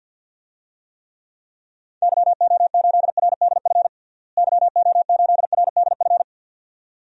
Morse code, heard as a keyed tone, about 51 wpm, 700 Hz. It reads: YO8RDF YO8RDF
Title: DXCC-100 Call Signs - 6 Characters - Repeated 2x - 50wpm